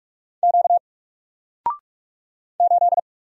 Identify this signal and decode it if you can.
Morse code, heard as a keyed tone, about 44 wpm, 700 Hz. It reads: Q 8
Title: Single Character (Letter:Number:Punctuation:Prosign) 45wpm